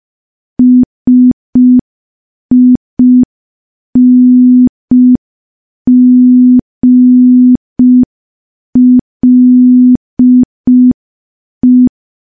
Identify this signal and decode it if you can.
Morse code, heard as a keyed tone, about 5 wpm, 250 Hz. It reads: SINGLE
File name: morse / morse-training-05WPM